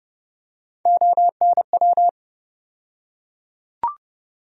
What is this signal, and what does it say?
Morse code, heard as a keyed tone, about 30 wpm, 700 Hz. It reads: ONW
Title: USA Call Sign Postfixes - 3 Letters 30wpm